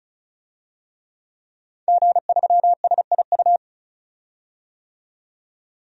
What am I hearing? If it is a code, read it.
Morse code, heard as a keyed tone, about 36 wpm, 700 Hz. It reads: G3SIU
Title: DXCC-100 Call Signs - 5 Characters - 35wpm